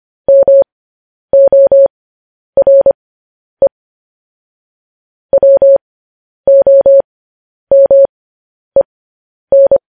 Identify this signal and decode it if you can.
Morse code, heard as a keyed tone, about 25 wpm, spaced as for 10 wpm, 550 Hz. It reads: MORE WOMEN
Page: news headlines in morse code at 10 wpm